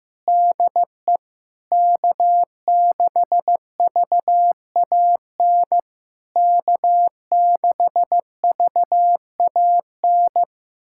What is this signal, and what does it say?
Morse code, heard as a keyed tone, about 15 wpm, 700 Hz. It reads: DE K6VAN K6VAN
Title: SOTA QSO Practice 15wpm